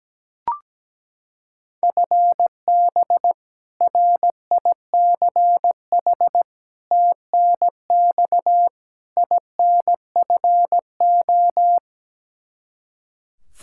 Morse code, heard as a keyed tone, about 17 wpm, 700 Hz. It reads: FB RICH TNX INFO
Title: Practice QSO - Exchange 3 - More Info - 17wpm